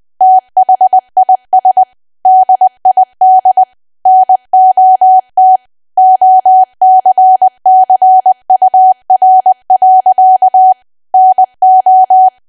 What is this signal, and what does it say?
Morse code, heard as a keyed tone, about 20 wpm, 750 Hz. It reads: THIS DID NOT OCCUR. NO
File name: cwa / W1AW-Level3 / 20WPM